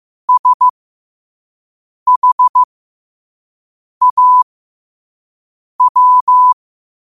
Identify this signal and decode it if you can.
Morse code, heard as a keyed tone, about 15 wpm, spaced as for 5 wpm, 1000 Hz. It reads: SHAW
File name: qotdfiles / 5wpm